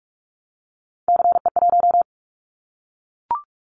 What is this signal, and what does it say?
Morse code, heard as a keyed tone, about 44 wpm, 700 Hz. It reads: CE1
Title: Callsign Prefixes for Top 100 DXCC - 45wpm